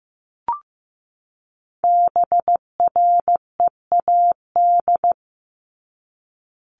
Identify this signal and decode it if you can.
Morse code, heard as a keyed tone, about 15 wpm, 700 Hz. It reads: BREAD
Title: Top 1501-1750 Words 15wpm